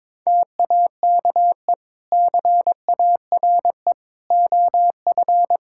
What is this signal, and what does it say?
Morse code, heard as a keyed tone, about 22 wpm, 700 Hz. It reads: TAKE CARE OF
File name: practice / 22wpm / Sentences with 2-Word Phrases - 22wpm